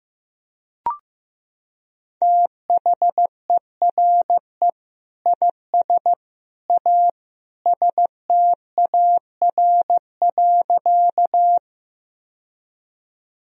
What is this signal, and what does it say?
Morse code, heard as a keyed tone, about 16 wpm, 700 Hz. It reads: THERE IS A STAR.
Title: Sentences from Top 1000 Words - Review Part 2 - 15wpm